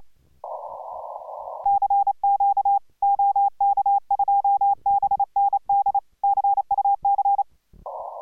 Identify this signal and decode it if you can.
Morse code, heard as a keyed tone, about 28 wpm, 810 Hz. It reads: CQ OK2BND CUC